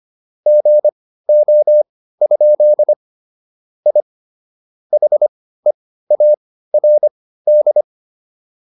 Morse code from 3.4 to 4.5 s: I

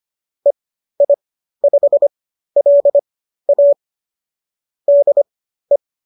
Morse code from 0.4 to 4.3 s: EI5LA

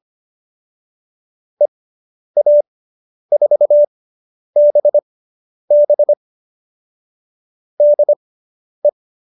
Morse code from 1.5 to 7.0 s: EA4BB

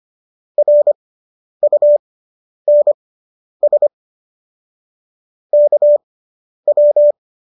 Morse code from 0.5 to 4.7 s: RUNS

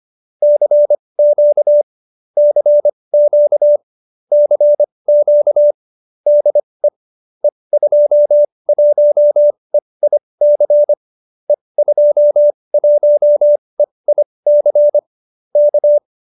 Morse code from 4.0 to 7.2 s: CQ DE

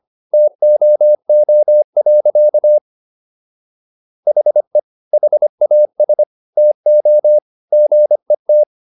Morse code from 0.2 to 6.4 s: TOO. HE HAS

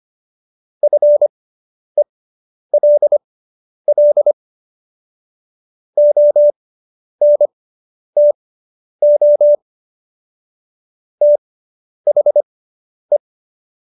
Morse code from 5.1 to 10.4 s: ONTO

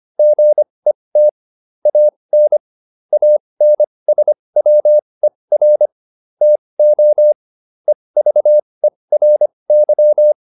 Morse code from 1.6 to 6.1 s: AN ANSWER

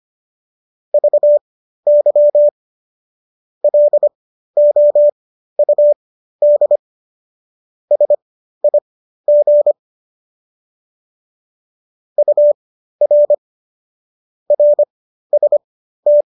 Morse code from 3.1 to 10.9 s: LOUD SIG

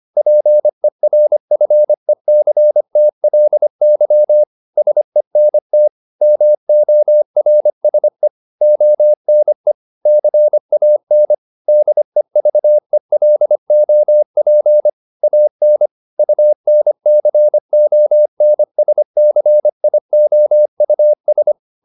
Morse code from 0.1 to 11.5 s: PERFECTLY SENT MORSE ONE CAN

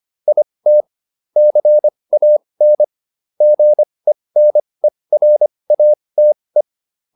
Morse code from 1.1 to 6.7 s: CAN GENERATE

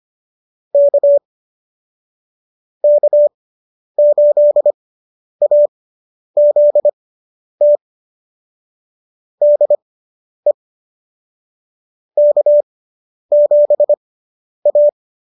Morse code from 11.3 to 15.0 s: K7A

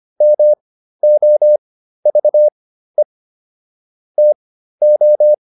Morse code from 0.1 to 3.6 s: MOVE

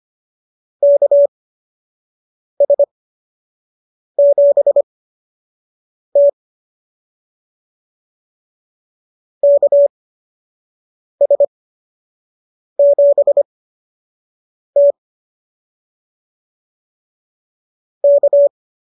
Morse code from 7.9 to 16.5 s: KS7T

